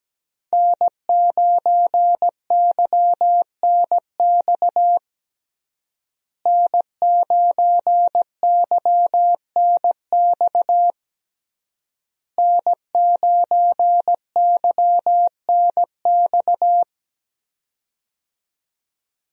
Morse code, heard as a keyed tone, about 17 wpm, 700 Hz. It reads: N9YNX N9YNX N9YNX
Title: USA Call Signs - 5 Characters - Repeated 3x - 17wpm